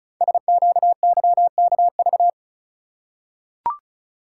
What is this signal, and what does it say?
Morse code, heard as a keyed tone, about 36 wpm, 690 Hz. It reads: SQYKV
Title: Code Groups - 5 Letters - 35wpm